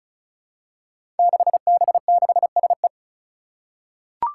Morse code, heard as a keyed tone, about 36 wpm, 700 Hz. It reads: 6B6SE
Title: Five Challenging Letter-Number Combinations 35wpm